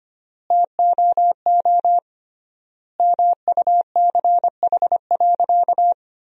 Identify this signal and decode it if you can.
Morse code, heard as a keyed tone, about 25 wpm, 700 Hz. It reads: TOO MUCH.
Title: 3x Word-Spacing - Sentences from Top 300 Words - 25wpm